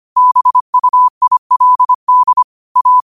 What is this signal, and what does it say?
Morse code, heard as a keyed tone, about 25 wpm, 1000 Hz. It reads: BUILD A
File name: qotdfiles / 25wpm